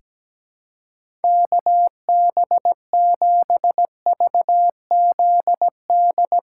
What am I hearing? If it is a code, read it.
Morse code, heard as a keyed tone, about 17 wpm, 700 Hz. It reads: KB7VZD
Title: USA Callsign DE Callsign 17wpm